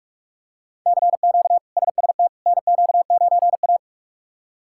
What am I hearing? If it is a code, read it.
Morse code, heard as a keyed tone, about 46 wpm, 700 Hz. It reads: CQ SST NQ9A
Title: K1USN SST QSO Practice - 45wpm